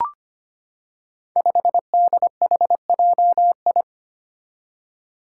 Morse code, heard as a keyed tone, about 25 wpm, 700 Hz. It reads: 5DHJI
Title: Five Challenging Letter-Number Combinations 25wpm